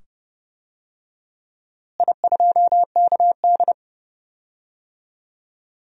Morse code, heard as a keyed tone, about 30 wpm, 700 Hz. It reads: I2KD
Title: DXCC-100 Call Signs - 4 Characters - Repeated 2x - 30wpm